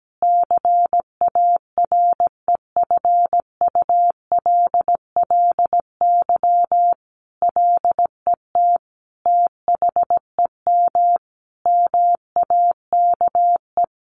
Morse code, heard as a keyed tone, about 17 wpm, 700 Hz. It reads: CAREFULLY LET THEM MAKE
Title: Sentences from Top 1300 Words 17wpm